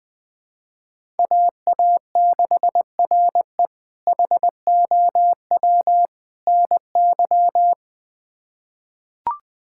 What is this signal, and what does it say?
Morse code, heard as a keyed tone, about 20 wpm, 700 Hz. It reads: AA6RE HOW NY
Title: K1USN SST QSO Practice - 20wpm